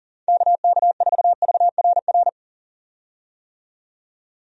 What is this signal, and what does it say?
Morse code, heard as a keyed tone, about 40 wpm, 700 Hz. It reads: KK4VRR
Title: USA Call Signs - 6 Characters - 40wpm